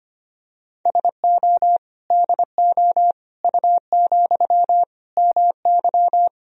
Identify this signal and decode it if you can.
Morse code, heard as a keyed tone, about 25 wpm, 700 Hz. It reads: SO DO U, MY
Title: Sentences from Top 400 Words and Abbreviations 25wpm